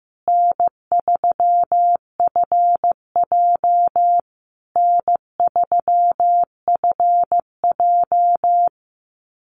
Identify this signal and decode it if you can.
Morse code, heard as a keyed tone, about 15 wpm, 700 Hz. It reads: N3FJ N3FJ
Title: USA Call Signs Repeated 3x 15wpm